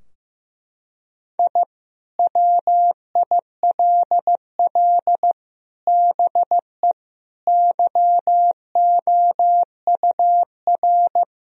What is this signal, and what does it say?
Morse code, heard as a keyed tone, about 16 wpm, 700 Hz. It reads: I WILL BE YOUR
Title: Sentences from Top 1000 Words - Review Part 4 - 15wpm